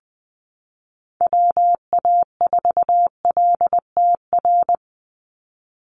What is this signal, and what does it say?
Morse code, heard as a keyed tone, about 20 wpm, 700 Hz. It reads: WA4LTR
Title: USA Call Signs 20wpm